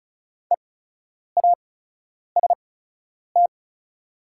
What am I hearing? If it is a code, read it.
Morse code, heard as a keyed tone, about 36 wpm, spaced as for 10 wpm, 700 Hz. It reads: EAST
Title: 8x Character Spacing - Top 500 Words - 35wpm